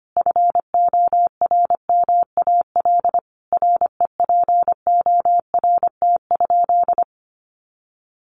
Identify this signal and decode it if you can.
Morse code, heard as a keyed tone, about 25 wpm, 700 Hz. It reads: FORMAL REPORT?
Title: Sentences from Top 1300 Words 25wpm